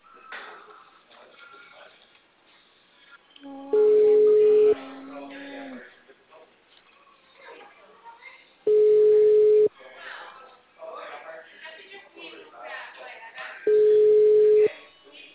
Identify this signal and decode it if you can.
Ringback tone (Europe, 425 Hz), with speech.